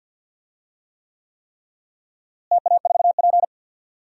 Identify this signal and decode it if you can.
Morse code, heard as a keyed tone, about 47 wpm, 700 Hz. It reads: TA4P
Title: DXCC-100 Call Signs - 4 Characters - Repeated 3x - 50wpm